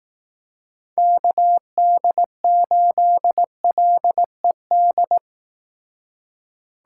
Morse code, heard as a keyed tone, about 18 wpm, 700 Hz. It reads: KD8LED
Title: Speed Racing - USA 2x3 Call Signs 15wpm